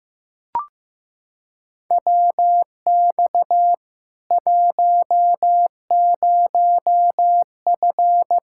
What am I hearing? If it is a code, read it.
Morse code, heard as a keyed tone, about 16 wpm, 700 Hz. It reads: WX 10F
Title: Practice QSO - Exchange 2 - Info - 15wpm